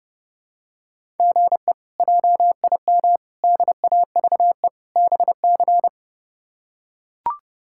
Morse code, heard as a keyed tone, about 32 wpm, 700 Hz. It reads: GE JIM DAVE BC
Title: K1USN SST QSO Practice - 30wpm